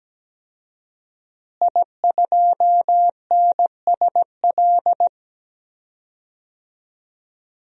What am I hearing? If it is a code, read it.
Morse code, heard as a keyed tone, about 17 wpm, 700 Hz. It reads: I2NSL
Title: DXCC-100 Call Signs - 5 Characters - Repeated 2x - 17wpm